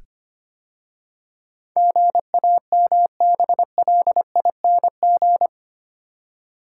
Morse code, heard as a keyed tone, about 25 wpm, 700 Hz. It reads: GAMBLING
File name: 8 Letter Words - First 3 Letters Given - 25wpm